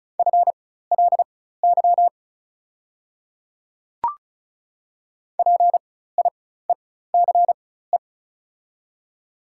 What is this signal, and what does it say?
Morse code, heard as a keyed tone, about 36 wpm, spaced as for 16 wpm, 700 Hz. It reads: FLY PIECE
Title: 4x Character Spacing - Top 500 Words - 35wpm